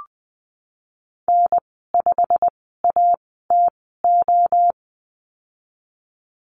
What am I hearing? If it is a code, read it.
Morse code, heard as a keyed tone, about 20 wpm, spaced as for 15 wpm, 700 Hz. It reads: N5ATO
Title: 186 Lesson 30g - Letter b - Call Signs 20wpm